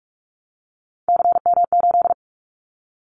Morse code, heard as a keyed tone, about 46 wpm, 700 Hz. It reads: CM8
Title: Callsign Prefixes for Top 100 DXCC - 45wpm